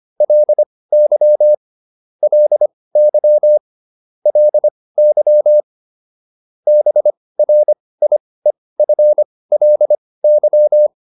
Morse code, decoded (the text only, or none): LY LY LY BRIEFLY